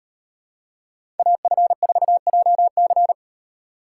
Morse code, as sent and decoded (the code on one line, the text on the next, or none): .- ..-. ....- .--- -.-.
AF4JC